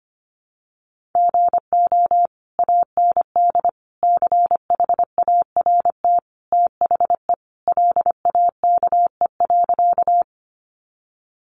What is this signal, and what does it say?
Morse code, heard as a keyed tone, about 25 wpm, 700 Hz. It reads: GO AND CHART THE LAKE.
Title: Sentences from Top 1300 Words 25wpm